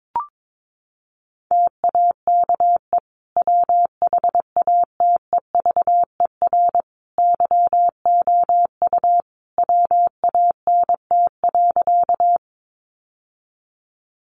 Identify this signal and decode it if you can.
Morse code, heard as a keyed tone, about 22 wpm, 700 Hz. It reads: TAKE WHATEVER YOU WANT.
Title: Sentences from Top 1100 Words 22wpm